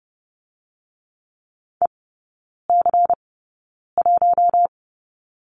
Morse code, heard as a keyed tone, about 30 wpm, spaced as for 9 wpm, 700 Hz. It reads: EC1
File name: Lesson 17e - Letter u - Three Characters 30wpm